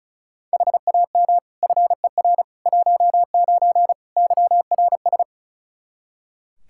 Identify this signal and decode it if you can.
Morse code, heard as a keyed tone, about 36 wpm, 700 Hz. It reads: HAM FER 19 YRS
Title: K1USN SST QSO Practice 35wpm